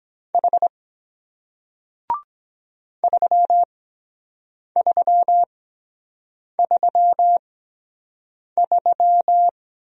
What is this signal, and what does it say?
Morse code, heard as a keyed tone, about 23 wpm, spaced as for 7 wpm, 700 Hz. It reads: H 3333